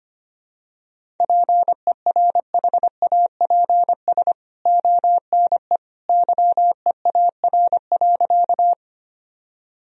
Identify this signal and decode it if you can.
Morse code, heard as a keyed tone, about 25 wpm, 700 Hz. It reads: PERHAPS ONE YEAR.
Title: Sentences from Top 1000 Words - Review Part 1 - 25wpm